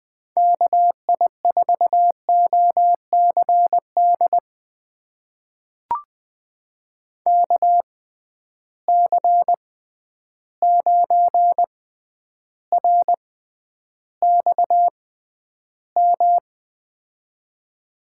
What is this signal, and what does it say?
Morse code, heard as a keyed tone, about 20 wpm, spaced as for 10 wpm, 700 Hz. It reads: KI4OCD KC9RXM
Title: USA Call Signs - 6 Characters - 6x Character Spacing - 20wpm